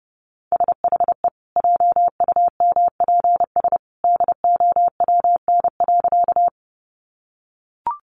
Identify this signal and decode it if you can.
Morse code, heard as a keyed tone, about 30 wpm, 700 Hz. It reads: SHE JUMPS DOWN.
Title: Easy Phrases 30wpm